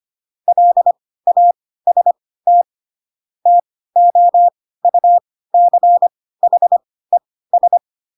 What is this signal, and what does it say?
Morse code, heard as a keyed tone, about 25 wpm, spaced as for 16 wpm, 700 Hz. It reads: LAST TOUCHES
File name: Pinocchio / Pinocchio ch3 10 25 ik1hge